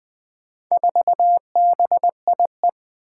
Morse code, heard as a keyed tone, about 20 wpm, 700 Hz. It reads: 4BIE
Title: Four Challenging Letter-Number Combinations 20wpm